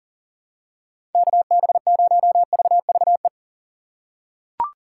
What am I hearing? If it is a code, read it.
Morse code, heard as a keyed tone, about 40 wpm, 700 Hz. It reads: KB0VVE
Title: USA Call Signs - 6 Characters - 3x Character Spacing - 40wpm